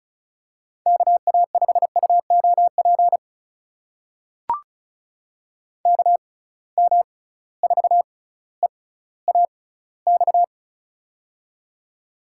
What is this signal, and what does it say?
Morse code, heard as a keyed tone, about 35 wpm, spaced as for 18 wpm, 700 Hz. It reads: KA5UOP KM4EAX
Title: USA Call Signs - 6 Characters - 6x Character Spacing - 35wpm